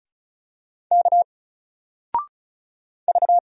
Morse code, heard as a keyed tone, about 34 wpm, 700 Hz. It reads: K V
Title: Single Letter-Number 35wpm